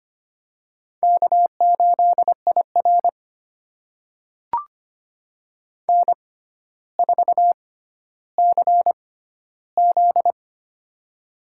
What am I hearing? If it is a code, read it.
Morse code, heard as a keyed tone, about 25 wpm, spaced as for 13 wpm, 700 Hz. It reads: K8IR N4CZ
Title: USA Call Signs - 4 Characters - 6x Character Spacing - 25wpm